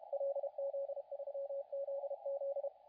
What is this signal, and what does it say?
Morse code, heard as a keyed tone, about 32 wpm, 560 Hz, with radio noise. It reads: LZ3ZZ
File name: la4c / sac-cw-2017 / CONTEST AUDIO 2100Z